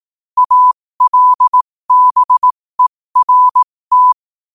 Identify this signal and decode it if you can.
Morse code, heard as a keyed tone, about 18 wpm, spaced as for 15 wpm, 1000 Hz. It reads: ALBERT
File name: qotdfiles / 15wpm